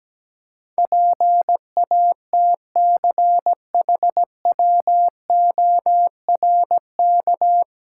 Morse code, heard as a keyed tone, about 17 wpm, 700 Hz. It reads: PATCHWORK
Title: Pangram Challenge v2 - 17wpm